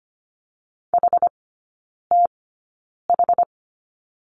Morse code, heard as a keyed tone, about 25 wpm, spaced as for 9 wpm, 700 Hz. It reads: HTH
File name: Lesson 17e - Letter u - Three Characters 25wpm